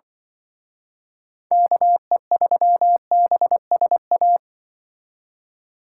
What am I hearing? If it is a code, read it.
Morse code, heard as a keyed tone, about 24 wpm, 700 Hz. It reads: KE3BSA
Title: Speed Racing - USA 2x3 Call Signs 20wpm